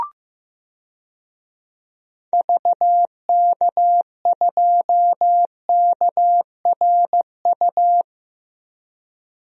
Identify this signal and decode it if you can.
Morse code, heard as a keyed tone, about 15 wpm, 700 Hz. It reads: VK2KRU
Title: DXCC-100 Call Signs - 6 Characters - Repeated 2x - 15wpm